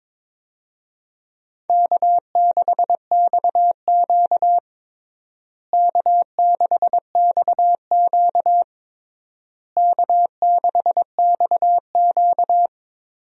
Morse code, heard as a keyed tone, about 22 wpm, 700 Hz. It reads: K6XQ K6XQ K6XQ
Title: USA Call Signs - 4 Characters - Repeated 3x - 22wpm